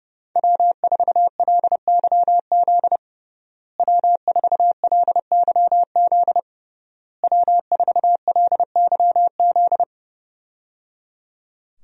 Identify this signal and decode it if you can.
Morse code, heard as a keyed tone, about 30 wpm, 700 Hz. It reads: W4LYZ W4LYZ W4LYZ
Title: USA Call Signs - 5 Characters - Repeated 3x - 30wpm